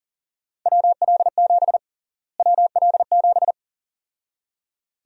Morse code, heard as a keyed tone, about 38 wpm, 700 Hz. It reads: WL7 WL7
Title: USA Call Sign Prefixes - Repeated 2x - 40wpm